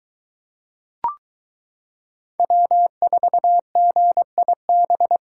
Morse code, heard as a keyed tone, about 23 wpm, 700 Hz. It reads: W4GIB